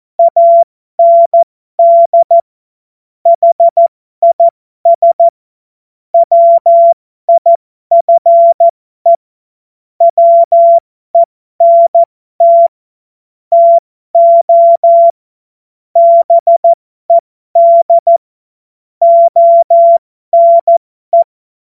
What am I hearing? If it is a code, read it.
Morse code, heard as a keyed tone, about 14 wpm, spaced as for 12 wpm, 680 Hz. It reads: AND HIS WIFE WENT TO BED ONE
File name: Morsekurs / Witze